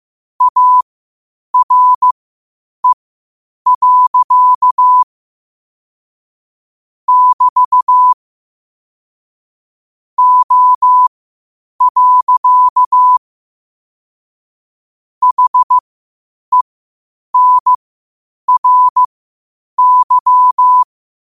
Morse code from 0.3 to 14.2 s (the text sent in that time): ARE. <BT> O.